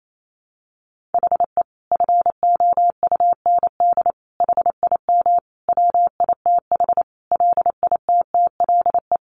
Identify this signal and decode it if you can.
Morse code, heard as a keyed tone, about 28 wpm, 700 Hz. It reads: HE FOUND HIM WITH LITTLE
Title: Sentences from Top 1500 Words 28wpm